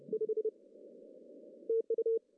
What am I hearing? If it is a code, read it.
Morse code, heard as a keyed tone, about 36 wpm, 470 Hz. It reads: 5 TU
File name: CW / cqww-cw-k5tr